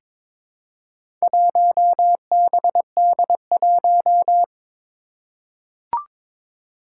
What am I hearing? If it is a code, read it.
Morse code, heard as a keyed tone, about 22 wpm, 700 Hz. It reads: 1BD1
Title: Four Challenging Letter-Number Combinations 22wpm